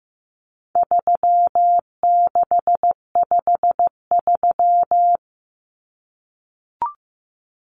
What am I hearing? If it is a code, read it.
Morse code, heard as a keyed tone, about 15 wpm, 700 Hz. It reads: 3653
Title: Common Cut Numbers - 4-Digit 15wpm